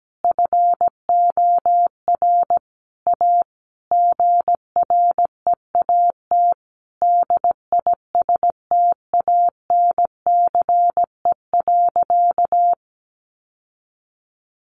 Morse code, heard as a keyed tone, about 17 wpm, 700 Hz. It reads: FOR A GREAT DISTANCE.
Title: Sentences from Top 1200 Words 17wpm